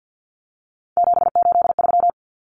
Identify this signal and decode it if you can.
Morse code, heard as a keyed tone, about 50 wpm, 700 Hz. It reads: 783
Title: Common Cut Numbers - 3-Digit 50wpm